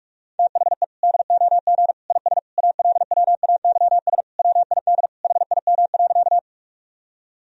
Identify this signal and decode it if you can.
Morse code, heard as a keyed tone, about 46 wpm, 700 Hz. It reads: THE DOG IS ALWAYS WID HIM.
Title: Sentences from Top 400 Words and Abbreviations 45wpm